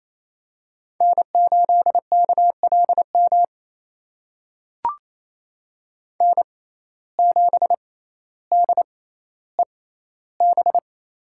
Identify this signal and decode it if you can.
Morse code, heard as a keyed tone, about 27 wpm, spaced as for 14 wpm, 700 Hz. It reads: N8KLM N7DEB
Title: USA Call Signs - 5 Characters - 6x Character Spacing - 28wpm